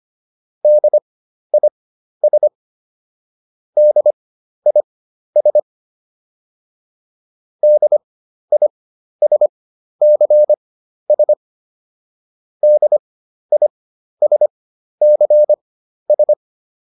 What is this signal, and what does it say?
Morse code, heard as a keyed tone, about 25 wpm, spaced as for 12 wpm, 600 Hz. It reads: DIS DIS DISCS DISCS